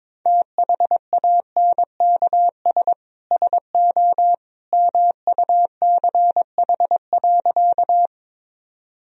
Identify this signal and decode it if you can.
Morse code, heard as a keyed tone, about 22 wpm, 700 Hz. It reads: THANKS SO MUCH.
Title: Sentences from Top 1000 Words - Review Part 3 - 22wpm